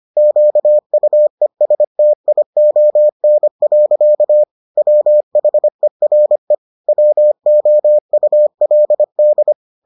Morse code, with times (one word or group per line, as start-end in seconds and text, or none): 0.2-4.4 QUESTION.
4.8-6.5 WHERE
6.9-9.5 WOULD